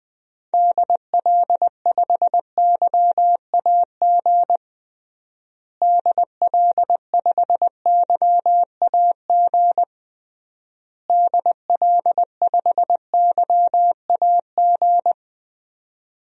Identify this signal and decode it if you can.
Morse code, heard as a keyed tone, about 20 wpm, 700 Hz. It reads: DL5YAG DL5YAG DL5YAG
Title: DXCC-100 Call Signs - 6 Characters - Repeated 3x - 20wpm